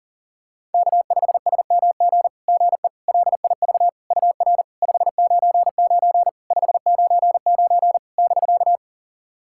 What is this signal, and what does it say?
Morse code, heard as a keyed tone, about 42 wpm, 700 Hz. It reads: K5SMG GE LIV UR 599 599 <BK>
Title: SOTA QSO Practice 40wpm